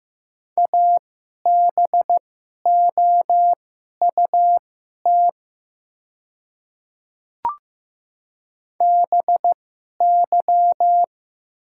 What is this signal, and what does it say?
Morse code, heard as a keyed tone, about 15 wpm, spaced as for 11 wpm, 700 Hz. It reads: ABOUT BY
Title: 2x Character Spacing - Top 200 Words - 15wpm